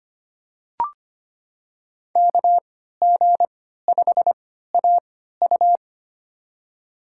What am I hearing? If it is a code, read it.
Morse code, heard as a keyed tone, about 25 wpm, spaced as for 14 wpm, 700 Hz. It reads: KG5AU
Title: USA Call Signs - 5 Characters - 3x Character Spacing - 25wpm